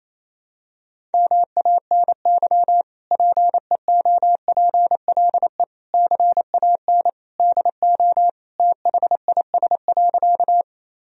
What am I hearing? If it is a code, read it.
Morse code, heard as a keyed tone, about 28 wpm, 700 Hz. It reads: MANY PEOPLE CAN DO THIS.
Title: Sentences from Top 1000 Words - Review Part 1 - 28wpm